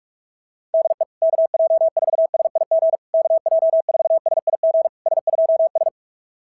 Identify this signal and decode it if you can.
Morse code, heard as a keyed tone, about 46 wpm, 650 Hz. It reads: DE KJ4SIG KJ4SIG S2S
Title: SOTA QSO Practice 45wpm